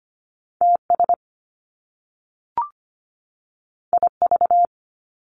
Morse code, heard as a keyed tone, about 25 wpm, 700 Hz. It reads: TS IV
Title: Top Two Letter Combinations in Words 25wpm